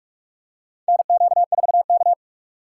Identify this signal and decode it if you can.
Morse code, heard as a keyed tone, about 46 wpm, 700 Hz. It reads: NQ4K
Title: USA Call Signs - 4 Characters - 2x Character Spacing - 45wpm